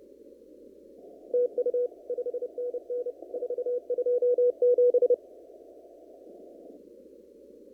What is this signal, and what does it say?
Morse code, heard as a keyed tone, about 29 wpm, 490 Hz. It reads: TU 5NN 427
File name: CW / kt5j wpx cw-op-k5pi